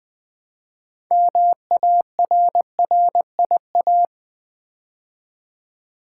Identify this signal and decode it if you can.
Morse code, heard as a keyed tone, about 20 wpm, 700 Hz. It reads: MARRIA
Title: Build A Word - 3 Letters At A Time - 8-Letter Words - 20wpm